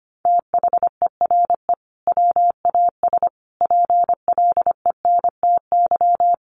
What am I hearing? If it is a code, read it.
Morse code, heard as a keyed tone, about 25 wpm, 700 Hz. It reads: THERE WAS PLENTY
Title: Sentences from Top 1300 Words 25wpm